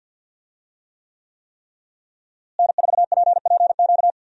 Morse code, heard as a keyed tone, about 46 wpm, 700 Hz. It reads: N4PPQ
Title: USA Call Signs - 5 Characters - Repeated 3x - 50wpm